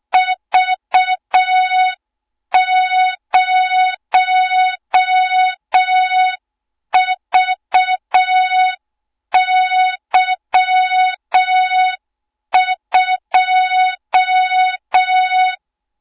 Morse code, heard as a keyed tone, about 6 wpm, 3700 Hz. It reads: V0VY2
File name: Morze-HA2MN / hang / 20 GYAKORLAT